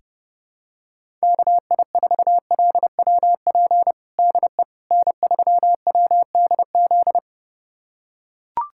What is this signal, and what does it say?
Morse code, heard as a keyed tone, about 32 wpm, 690 Hz. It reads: KI4LWP DE N3WDZ
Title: USA Callsign DE Callsign 30wpm